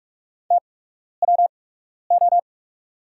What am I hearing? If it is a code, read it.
Morse code, heard as a keyed tone, about 46 wpm, spaced as for 12 wpm, 690 Hz. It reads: TWO
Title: 8x Character Spacing - Top 100 Words - 45wpm